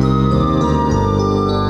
There is no code.